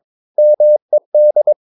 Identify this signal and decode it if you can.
Morse code, heard as a keyed tone, about 22 wpm, 600 Hz. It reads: MED
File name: Ferie QRS 22wpm